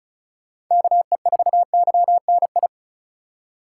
Morse code, heard as a keyed tone, about 43 wpm, 700 Hz. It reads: KE4YNI
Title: Speed Racing - USA 2x3 Call Signs 35wpm